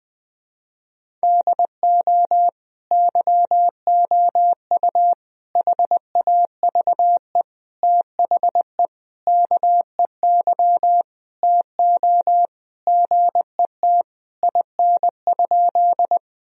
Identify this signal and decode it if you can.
Morse code, heard as a keyed tone, about 21 wpm, 700 Hz. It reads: DO YOU HAVE THE KEY TO GET IN?
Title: Sentences from Top 1000 Words - Review Part 4 - 20wpm